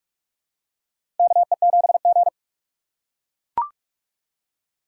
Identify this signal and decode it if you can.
Morse code, heard as a keyed tone, about 44 wpm, 700 Hz. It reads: KE7G
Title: USA Call Signs - 4 Characters - 4x Character Spacing - 45wpm